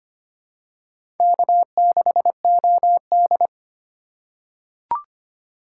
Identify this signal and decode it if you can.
Morse code, heard as a keyed tone, about 25 wpm, 700 Hz. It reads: K6OD
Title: USA Call Signs - 4x Character Spacing - 25wpm